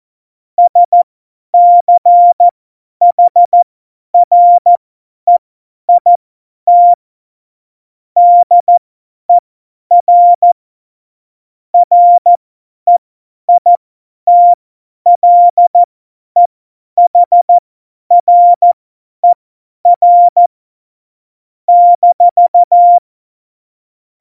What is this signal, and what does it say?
Morse code, heard as a keyed tone, about 14 wpm, spaced as for 10 wpm, 700 Hz. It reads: SCHREIT DER REITLEHRER -
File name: Morsekurs / Witze / Witz 04 August 2020, 50 F70